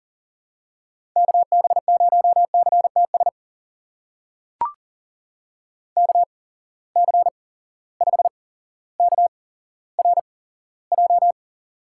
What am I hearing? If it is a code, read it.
Morse code, heard as a keyed tone, about 40 wpm, spaced as for 17 wpm, 700 Hz. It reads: KB0CTS KC5KRJ
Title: USA Call Signs - 6 Characters - 8x Character Spacing - 40wpm